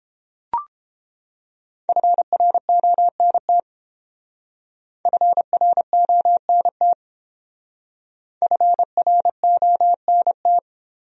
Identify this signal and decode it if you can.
Morse code, heard as a keyed tone, about 29 wpm, 700 Hz. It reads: FRONT FRONT FRONT